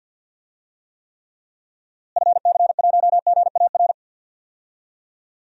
Morse code, heard as a keyed tone, about 51 wpm, 700 Hz. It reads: FC1GAR
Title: DXCC-100 Call Signs - 6 Characters - Repeated 3x - 50wpm